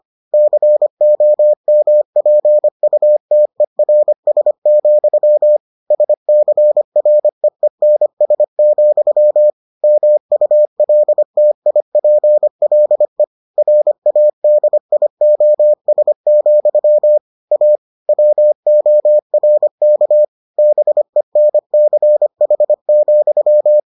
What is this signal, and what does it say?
Morse code, heard as a keyed tone, about 25 wpm, 600 Hz. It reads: COMPUTERS, SCREENS, MULTIPLE RADIOS, A WORK BENCH,